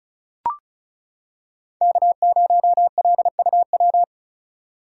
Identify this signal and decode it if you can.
Morse code, heard as a keyed tone, about 34 wpm, 700 Hz. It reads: K0LUW
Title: K1USN SST QSO Practice - 35wpm